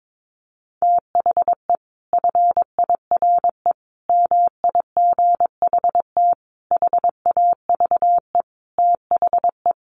Morse code, heard as a keyed tone, about 22 wpm, 700 Hz. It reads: THE FIRE MIGHT HAVE THE
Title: Sentences from Top 1300 Words 22wpm